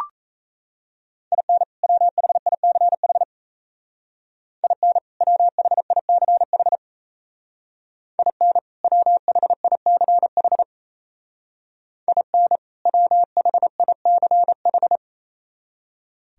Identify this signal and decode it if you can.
Morse code, heard as a keyed tone, about 36 wpm, 700 Hz. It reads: IN WHICH IN WHICH IN WHICH IN WHICH